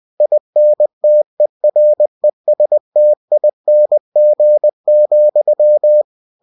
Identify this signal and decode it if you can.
Morse code, heard as a keyed tone, about 20 wpm, 600 Hz. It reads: INTERESTING,
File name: cw-books / winnie-the-pooh / winnie-20wpm-page-0101